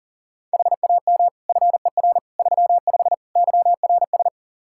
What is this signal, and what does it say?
Morse code, heard as a keyed tone, about 40 wpm, 700 Hz. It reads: HAM FER 35 YRS
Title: K1USN SST QSO Practice 40wpm